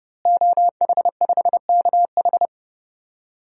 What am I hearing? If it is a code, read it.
Morse code, heard as a keyed tone, about 30 wpm, 700 Hz. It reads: OH5KH
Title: DXCC-100 Call Signs - 5 Characters - Repeated 2x - 30wpm